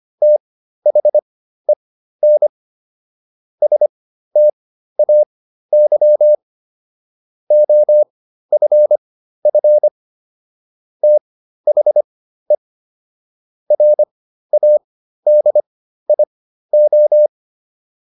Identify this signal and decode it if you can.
Morse code, heard as a keyed tone, about 25 wpm, spaced as for 13 wpm, 600 Hz. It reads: THEN STAY OFF THE RADIO